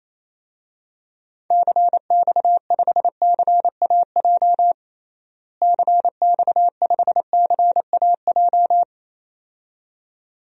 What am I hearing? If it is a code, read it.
Morse code, heard as a keyed tone, about 27 wpm, 700 Hz. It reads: CX5CAJ CX5CAJ
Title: DXCC-100 Call Signs - 6 Characters - Repeated 2x - 28wpm